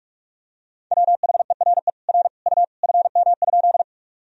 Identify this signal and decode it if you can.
Morse code, heard as a keyed tone, about 46 wpm, 700 Hz. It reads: WHERE R U FM?
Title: Sentences from Top 200 Words and Abbreviations 45wpm